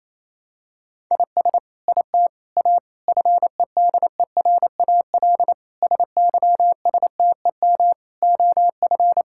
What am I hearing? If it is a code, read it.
Morse code, heard as a keyed tone, about 28 wpm, 700 Hz. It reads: IS IT A FEDERAL SYSTEM OF
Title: Sentences from Top 1000 Words - Review Part 5 - 28wpm